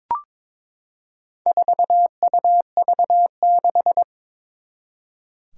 Morse code, heard as a keyed tone, about 22 wpm, 700 Hz. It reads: 4UV6
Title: Four Challenging Letter-Number Combinations 22wpm